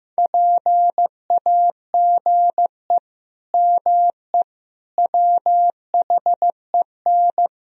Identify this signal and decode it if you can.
Morse code, heard as a keyed tone, about 15 wpm, 700 Hz. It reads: PAGE ME WHEN
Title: Sentences from Top 1000 Words - Review Part 1 - 15wpm